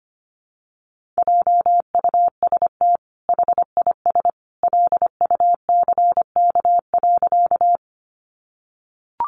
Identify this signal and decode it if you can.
Morse code, heard as a keyed tone, about 25 wpm, 700 Hz. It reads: JUST HIS LUCK.
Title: Sentences from Top 1500 Words 25wpm